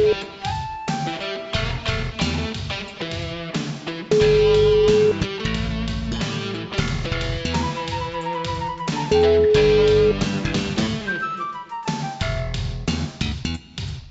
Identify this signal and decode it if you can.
Ringback tone (Europe, 425 Hz), with noise.